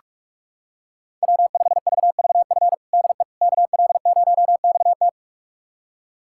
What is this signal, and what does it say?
Morse code, heard as a keyed tone, about 41 wpm, 700 Hz. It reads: W5VVF DE KL0XT
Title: USA Callsign DE Callsign - 5 Characters - 45wpm